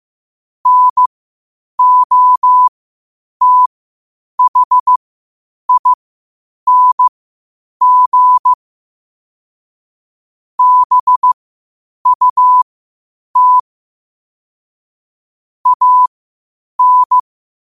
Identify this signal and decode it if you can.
Morse code, heard as a keyed tone, about 15 wpm, spaced as for 8 wpm, 1000 Hz. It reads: NOTHING BUT AN
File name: qotdfiles / 8wpm